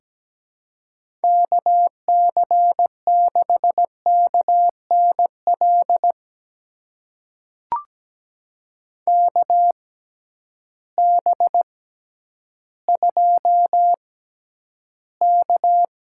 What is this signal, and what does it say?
Morse code, heard as a keyed tone, about 17 wpm, spaced as for 10 wpm, 700 Hz. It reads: KC6KNL KB2K
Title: USA Call Signs - 6 Characters - 6x Character Spacing - 17wpm